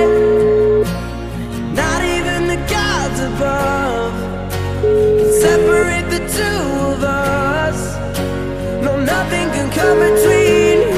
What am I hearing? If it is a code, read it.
Ringback tone (Europe, 425 Hz), with noise.